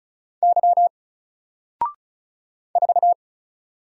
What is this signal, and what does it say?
Morse code, heard as a keyed tone, about 33 wpm, 700 Hz. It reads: Y 4
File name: Single-Letter-Number-35wpm